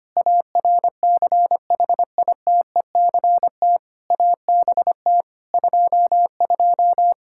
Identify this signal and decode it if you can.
Morse code, heard as a keyed tone, about 25 wpm, 700 Hz. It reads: ARCHITECT ABT 22
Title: Practice QSO - Exchange 3 - More Info - 25wpm